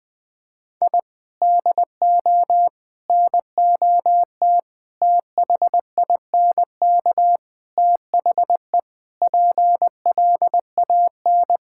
Morse code, heard as a keyed tone, about 20 wpm, 700 Hz. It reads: I DO NOT THINK THE PLAN
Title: Sentences with 5-Word Phrases - 20wpm